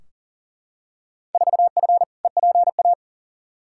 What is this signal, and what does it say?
Morse code, heard as a keyed tone, about 40 wpm, 700 Hz. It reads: 4F EPA
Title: Field Day Exchanges Encore - 40wpm